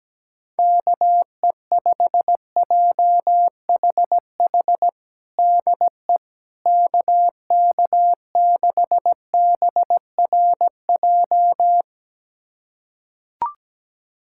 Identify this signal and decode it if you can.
Morse code, heard as a keyed tone, about 17 wpm, 700 Hz. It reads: KE5JHH DE KK6BRJ
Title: USA Callsign DE Callsign - 6 Characters - 17wpm